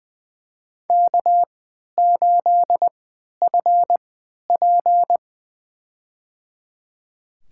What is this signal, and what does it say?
Morse code, heard as a keyed tone, about 21 wpm, spaced as for 11 wpm, 700 Hz. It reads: K8FP
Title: USA Call Signs - 3x Character Spacing - 20wpm